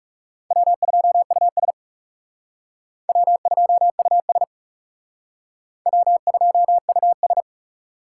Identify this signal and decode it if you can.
Morse code, heard as a keyed tone, about 37 wpm, 700 Hz. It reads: W2US W2US W2US
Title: Speed Racing - USA 1x2 Call Signs 30wpm